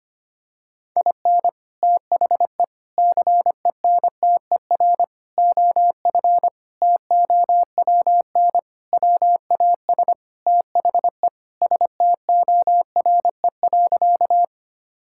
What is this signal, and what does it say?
Morse code, heard as a keyed tone, about 25 wpm, 700 Hz. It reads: IN THE CENTER OF TOWN WAS THE STORE.
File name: practice / 25wpm / Sentences with 4-Word Phrases - 25wpm